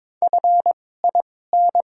Morse code, heard as a keyed tone, about 22 wpm, spaced as for 16 wpm, 700 Hz. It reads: FIN
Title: 2x Character Spacing - Top 500 Words - 22wpm